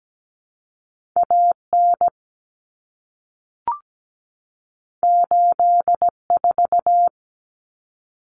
Morse code, heard as a keyed tone, about 17 wpm, 700 Hz. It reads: AN 84
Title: Common Cut Numbers - 2-Digit 17wpm